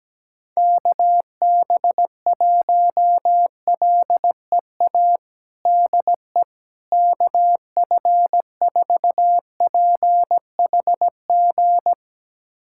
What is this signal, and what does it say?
Morse code, heard as a keyed tone, about 17 wpm, 700 Hz. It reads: KB1LEA DE KF4PHG
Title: USA Callsign DE Callsign 17wpm